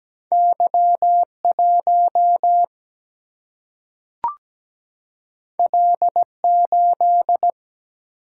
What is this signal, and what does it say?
Morse code, heard as a keyed tone, about 17 wpm, 700 Hz. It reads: Y1 L8
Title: Single-Letter Single-Number 17wpm